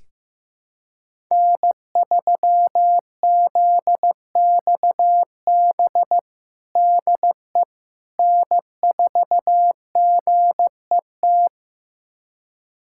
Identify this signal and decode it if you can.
Morse code, heard as a keyed tone, about 16 wpm, 700 Hz. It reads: N3ZXB DE N4GET
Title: USA Callsign DE Callsign - 5 Characters - 15wpm